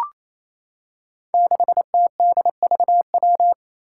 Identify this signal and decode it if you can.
Morse code, heard as a keyed tone, about 28 wpm, 700 Hz. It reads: 6TDVW
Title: Five Challenging Letter-Number Combinations 28wpm